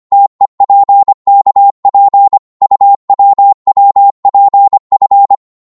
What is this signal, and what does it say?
Morse code, heard as a keyed tone, about 24 wpm, 800 Hz. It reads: TEPKP UWWPF
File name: Morse / ltr-25wpm